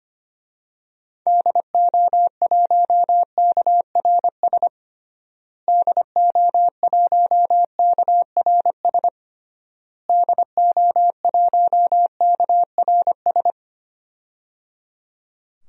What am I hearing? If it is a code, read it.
Morse code, heard as a keyed tone, about 25 wpm, 700 Hz. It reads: DO1KRS DO1KRS DO1KRS
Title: DXCC-100 Call Signs - 6 Characters - Repeated 3x - 25wpm